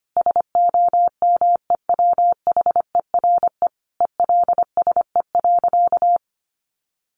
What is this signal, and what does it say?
Morse code, heard as a keyed tone, about 25 wpm, 700 Hz. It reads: SOMEWHERE ELSE.
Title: Sentences from Top 1400 Words 25wpm